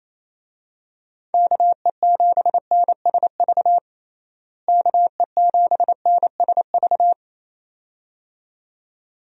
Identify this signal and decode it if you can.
Morse code, heard as a keyed tone, about 28 wpm, 700 Hz. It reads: KE7NSV KE7NSV
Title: USA Call Signs - 6 Characters - Repeated 2x - 28wpm